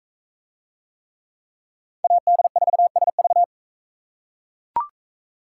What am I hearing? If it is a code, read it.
Morse code, heard as a keyed tone, about 42 wpm, 700 Hz. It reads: AD4SV